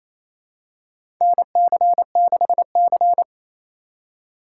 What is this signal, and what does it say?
Morse code, heard as a keyed tone, about 27 wpm, 700 Hz. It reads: NC6C
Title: USA Call Signs - 4 Characters - 3x Character Spacing - 28wpm